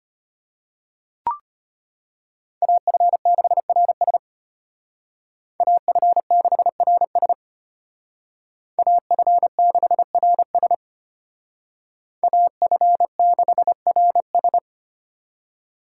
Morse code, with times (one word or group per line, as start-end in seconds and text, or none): 2.6-4.2 AF6RS
5.6-7.3 AF6RS
8.8-10.7 AF6RS
12.2-14.6 AF6RS